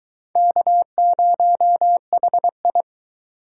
Morse code, heard as a keyed tone, about 18 wpm, 700 Hz. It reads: K0HI
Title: Speed Racing - USA 1x2 Call Signs 15wpm